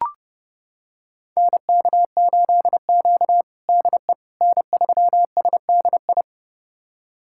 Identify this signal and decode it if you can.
Morse code, heard as a keyed tone, about 32 wpm, 700 Hz. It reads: NK8Q DE N3SDI
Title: USA Callsign DE Callsign 30wpm